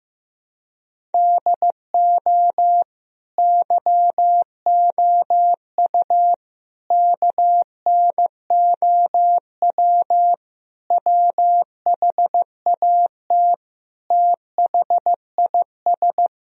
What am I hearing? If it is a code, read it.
Morse code, heard as a keyed tone, about 15 wpm, 700 Hz. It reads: DO YOU KNOW WHAT THIS
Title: Sentences from Top 1000 Words - Review Part 1 - 15wpm